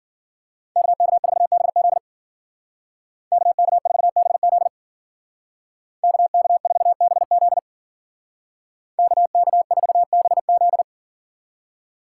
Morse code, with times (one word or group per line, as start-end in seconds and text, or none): 0.8-2.0 KK4BZ
3.3-4.7 KK4BZ
6.0-7.6 KK4BZ
9.0-10.8 KK4BZ